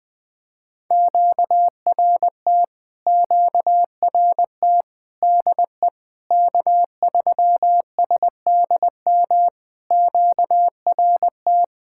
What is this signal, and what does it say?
Morse code, heard as a keyed tone, about 20 wpm, 700 Hz. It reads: QRT QRT DE K3SDM QRT
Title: POTA QSO Practice - 20wpm